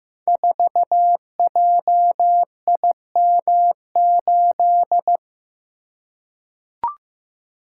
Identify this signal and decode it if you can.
Morse code, heard as a keyed tone, about 15 wpm, 700 Hz. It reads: 4JIM8
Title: Code Groups - 5 Letter:Numbers - 15wpm